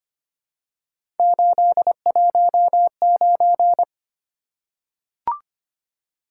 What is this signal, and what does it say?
Morse code, heard as a keyed tone, about 25 wpm, 700 Hz. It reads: 819
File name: Three Letter-Number Combinations - 25wpm